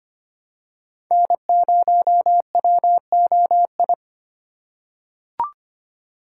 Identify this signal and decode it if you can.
Morse code, heard as a keyed tone, about 25 wpm, 700 Hz. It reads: N0WOI
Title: K1USN SST QSO Practice - 25wpm